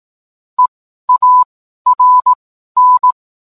Morse code, heard as a keyed tone, about 18 wpm, spaced as for 12 wpm, 1000 Hz. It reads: EARN
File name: cwblog / cw mp3 directory / mem morse 12